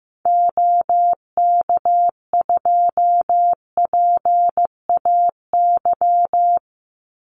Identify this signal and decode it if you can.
Morse code, heard as a keyed tone, about 15 wpm, 700 Hz. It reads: OK2PAY
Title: Call Signs Repeated 3x - 15wpm